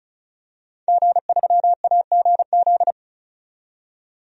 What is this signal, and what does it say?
Morse code, heard as a keyed tone, about 36 wpm, 700 Hz. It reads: G3AGZ
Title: DXCC-100 Call Signs - 5 Characters - 35wpm